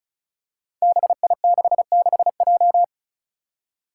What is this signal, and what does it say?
Morse code, heard as a keyed tone, about 34 wpm, 700 Hz. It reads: BI66J
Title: Five Challenging Letter-Number Combinations 35wpm